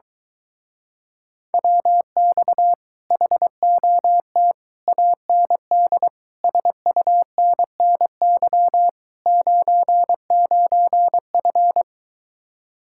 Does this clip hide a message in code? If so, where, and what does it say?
Yes: Morse code from 1.5 to 11.8 s, heard as a keyed tone, about 24 wpm, 700 Hz. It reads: WX HOT AND SUNNY 99F